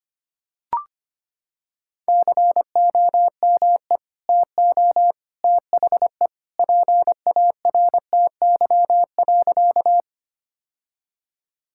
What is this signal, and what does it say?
Morse code, heard as a keyed tone, about 25 wpm, 700 Hz. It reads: COME TO THE PARTY.
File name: practice / 25wpm / Sentences with 2-Word Phrases - 25wpm